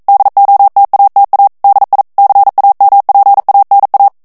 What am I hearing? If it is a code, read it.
Morse code, heard as a keyed tone, about 42 wpm, 800 Hz. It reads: DOTATA DI CAMPANA
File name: ITALIEN texte clair 1 vitesse 40 mots minute